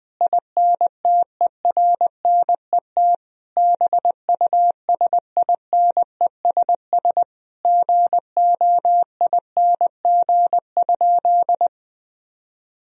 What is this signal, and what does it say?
Morse code, heard as a keyed tone, about 20 wpm, 700 Hz. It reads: INTERNET BUSINESS GOING?
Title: Sentences from Top 1000 Words - Review Part 4 - 20wpm